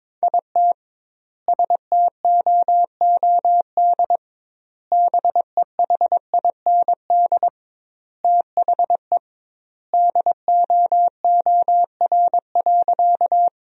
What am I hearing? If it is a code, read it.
Morse code, heard as a keyed tone, about 21 wpm, 700 Hz. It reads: IT STOOD BEHIND THE DOOR.
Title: 2x Word-Spacing - Sentences from Top 500 Words - 22wpm